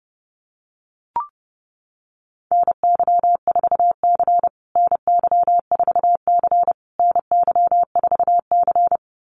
Morse code, heard as a keyed tone, about 28 wpm, 700 Hz. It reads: NY4C NY4C NY4C
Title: USA Call Signs Repeated 3x 30wpm